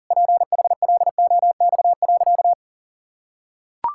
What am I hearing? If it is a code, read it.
Morse code, heard as a keyed tone, about 40 wpm, 700 Hz. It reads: PHLOX.
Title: Pangram Challenge v2 - 40wpm